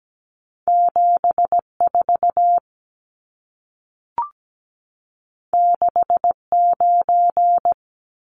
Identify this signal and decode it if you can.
Morse code, heard as a keyed tone, about 17 wpm, 700 Hz. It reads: 74 69
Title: Numbers - 2-Digit 17wpm